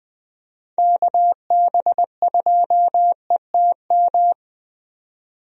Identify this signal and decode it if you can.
Morse code, heard as a keyed tone, about 20 wpm, 700 Hz. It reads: KB2ETM
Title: USA Call Signs - 6 Characters - 20wpm